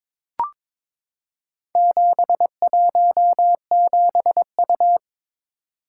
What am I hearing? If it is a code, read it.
Morse code, heard as a keyed tone, about 22 wpm, 690 Hz. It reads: 717U
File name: practice / 22wpm / Four Letter-Number Combinations - 22wpm